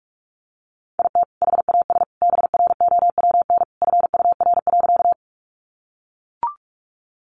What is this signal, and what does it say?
Morse code, heard as a keyed tone, about 44 wpm, 700 Hz. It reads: IT HAS BROWN FUR.
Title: Easy Phrases 45wpm